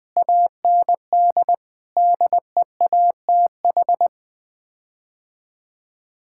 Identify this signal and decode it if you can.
Morse code, heard as a keyed tone, about 20 wpm, 700 Hz. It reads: AND DEATH
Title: Sets of 8 Words - 20wpm